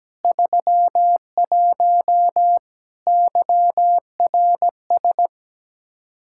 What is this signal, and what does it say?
Morse code, heard as a keyed tone, about 17 wpm, 700 Hz. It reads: 31 YRS
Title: K1USN SST QSO Practice 17wpm